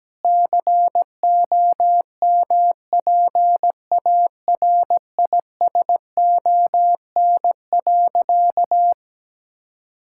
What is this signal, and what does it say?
Morse code, heard as a keyed tone, about 17 wpm, 700 Hz. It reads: COMPARISON.
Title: Sentences from Top 1600 Words 17wpm